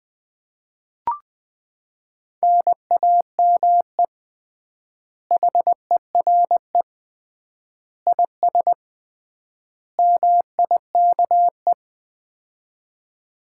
Morse code, heard as a keyed tone, about 21 wpm, 700 Hz. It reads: NAME HERE IS MIKE
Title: Common QSO Phrases - 3x Word-Spacing - 20wpm